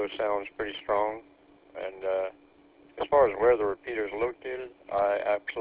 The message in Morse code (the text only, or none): none